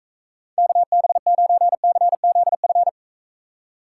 Morse code, heard as a keyed tone, about 41 wpm, 700 Hz. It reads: KB9CZF